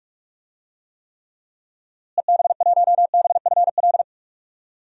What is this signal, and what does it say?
Morse code, heard as a keyed tone, about 46 wpm, 700 Hz. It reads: EB1BFL
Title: DXCC-100 Call Signs - 6 Characters - Repeated 2x - 45wpm